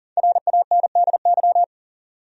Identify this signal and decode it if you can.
Morse code, heard as a keyed tone, about 40 wpm, 700 Hz. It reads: RANDY
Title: USA Common Names - Max 5 Letters - 40wpm